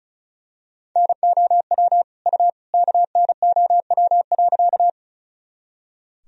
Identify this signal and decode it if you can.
Morse code, heard as a keyed tone, about 36 wpm, 700 Hz. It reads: NOW U KNOW.
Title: Sentences from Top 100 Words and Abbreviations 35wpm